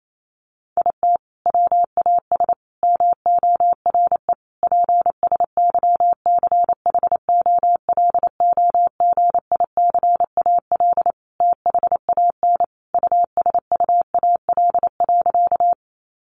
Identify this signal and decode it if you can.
Morse code, heard as a keyed tone, about 28 wpm, 700 Hz. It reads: IT WAS MORE PSYCHOLOGICAL THAN USUAL.
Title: Sentences from Top 1400 Words 28wpm